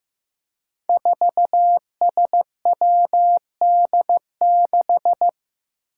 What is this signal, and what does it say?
Morse code, heard as a keyed tone, about 15 wpm, 700 Hz. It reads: 4SWD6
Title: Five Challenging Letter-Number Combinations 15wpm